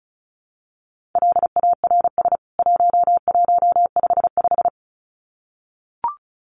Morse code, heard as a keyed tone, about 34 wpm, 700 Hz. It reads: LARS 1155
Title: CWT Exchange 35wpm